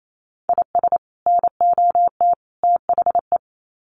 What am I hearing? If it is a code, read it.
Morse code, heard as a keyed tone, about 28 wpm, 700 Hz. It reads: IS NOT THE
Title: Sets of 3 Words 28wpm